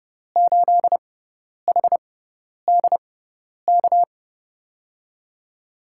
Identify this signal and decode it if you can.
Morse code, heard as a keyed tone, about 32 wpm, spaced as for 10 wpm, 700 Hz. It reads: 8HDK
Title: USA Call Signs - 6 Characters - 6x Character Spacing - 30wpm